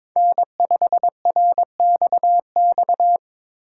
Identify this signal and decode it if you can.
Morse code, heard as a keyed tone, about 22 wpm, 700 Hz. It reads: N5RXX
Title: USA Call Signs - 4x Character Spacing - 22wpm